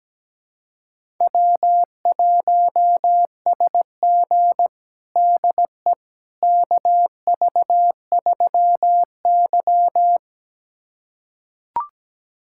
Morse code, heard as a keyed tone, about 17 wpm, 700 Hz. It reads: W1SG DE KV3Y
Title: USA Callsign DE Callsign - 4 Characters - 17wpm